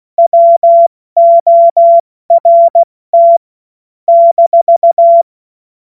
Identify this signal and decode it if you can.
Morse code, heard as a keyed tone, about 16 wpm, spaced as for 14 wpm, 680 Hz. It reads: WORT -
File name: Morsekurs / Witze / Witz 28. Dezember 2021, 70 F80